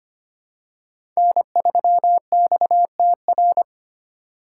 Morse code, heard as a keyed tone, about 25 wpm, 700 Hz. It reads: N3XTR
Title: USA Call Signs - 4x Character Spacing - 25wpm